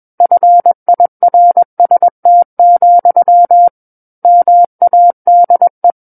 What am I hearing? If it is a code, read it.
Morse code, heard as a keyed tone, about 21 wpm, 700 Hz. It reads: FIRST, MADE
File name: Book One / 12-Chapter 12 - 21 WPM